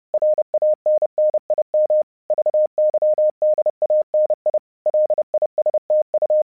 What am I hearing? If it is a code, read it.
Morse code, heard as a keyed tone, about 30 wpm, 610 Hz. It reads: RANNIM VYDANI LISTU
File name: vyuka cw 1 / 30wpm / hasek 30wpm uvod